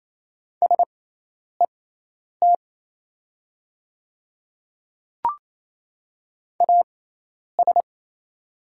Morse code, heard as a keyed tone, about 27 wpm, spaced as for 10 wpm, 700 Hz. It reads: SET AS
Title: 6x Character Spacing - Top 200 Words - 28wpm